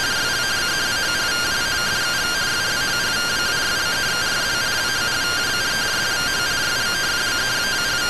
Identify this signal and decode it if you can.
Two-tone teleprinter signal, about 170 Hz shift.